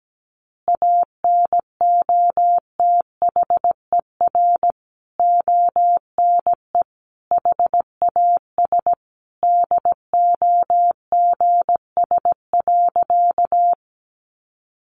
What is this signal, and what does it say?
Morse code, heard as a keyed tone, about 17 wpm, 700 Hz. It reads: ANOTHER ONE HAS DOGS.
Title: Additional Easy Phrases 17wpm